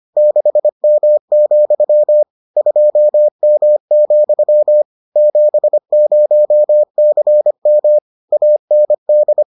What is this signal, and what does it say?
Morse code, heard as a keyed tone, about 25 wpm, 600 Hz. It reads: 6M, 2M, 70CM AND